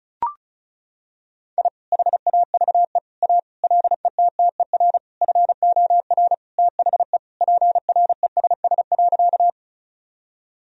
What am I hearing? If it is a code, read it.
Morse code, heard as a keyed tone, about 36 wpm, 700 Hz. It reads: I HAVE A LETTER FOR THE PRESS.
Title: Sentences from Top 1000 Words - Review Part 1 - 35wpm